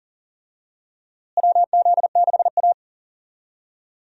Morse code, heard as a keyed tone, about 40 wpm, 700 Hz. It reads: WZ6A
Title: USA Call Signs - 4 Characters - Repeated 2x - 40wpm